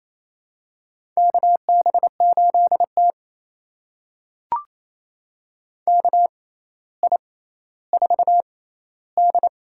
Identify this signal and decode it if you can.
Morse code, heard as a keyed tone, about 28 wpm, spaced as for 14 wpm, 700 Hz. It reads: KB8T KI4D
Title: USA Call Signs - 6x Character Spacing - 28wpm